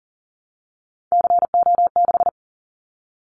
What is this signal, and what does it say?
Morse code, heard as a keyed tone, about 38 wpm, 700 Hz. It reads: CO6
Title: Callsign Prefixes for Top 100 DXCC - 40wpm